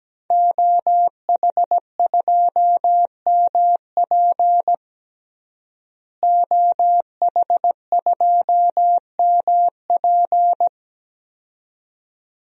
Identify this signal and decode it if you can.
Morse code, heard as a keyed tone, about 17 wpm, 700 Hz. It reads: OH2MP OH2MP
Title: DXCC-100 Call Signs - 5 Characters - Repeated 2x - 17wpm